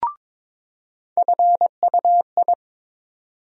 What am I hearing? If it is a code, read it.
Morse code, heard as a keyed tone, about 22 wpm, 700 Hz. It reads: FUI
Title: USA Call Sign Postfixes - 3 Letters 22wpm